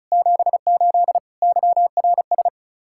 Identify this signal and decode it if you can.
Morse code, heard as a keyed tone, about 43 wpm, 700 Hz. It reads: 78 YRS
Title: Speed-Racing Common QSO Phrases Encore - 35wpm